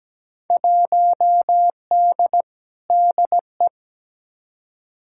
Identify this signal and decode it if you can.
Morse code, heard as a keyed tone, about 17 wpm, 700 Hz. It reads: 1D DE
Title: Field Day Exchanges - 17wpm